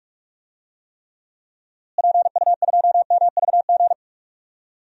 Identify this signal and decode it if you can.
Morse code, heard as a keyed tone, about 43 wpm, 700 Hz. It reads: PU2MVG
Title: DXCC-100 Call Signs - 6 Characters - Repeated 2x - 45wpm